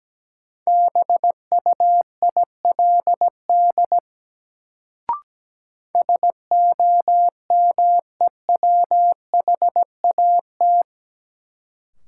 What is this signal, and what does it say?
Morse code, heard as a keyed tone, about 17 wpm, 700 Hz. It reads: BUILD SOMEWHAT
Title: Sentences from Top 1000 Words - Review Part 1 - New Word Drill - 17wpm